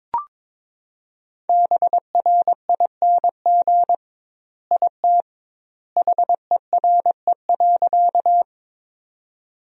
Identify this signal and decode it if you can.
Morse code, heard as a keyed tone, about 21 wpm, 700 Hz. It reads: BRING IT HERE.
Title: 2x Word-Spacing - Sentences from Top 500 Words - 22wpm